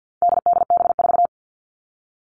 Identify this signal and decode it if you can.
Morse code, heard as a keyed tone, about 50 wpm, 700 Hz. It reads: DDB4
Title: All Cut Numbers - 4-Digit 50wpm